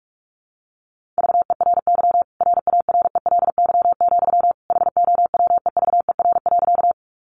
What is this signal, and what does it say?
Morse code, heard as a keyed tone, about 44 wpm, 700 Hz. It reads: VERY RARELY, HOWEVER.
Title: Sentences from Top 1400 Words 45wpm